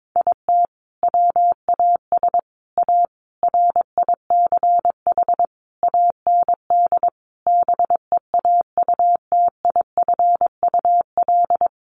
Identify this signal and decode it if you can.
Morse code, heard as a keyed tone, about 22 wpm, 700 Hz. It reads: IT WAS A RICH AND BEAUTIFUL
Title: Sentences from Top 1100 Words 22wpm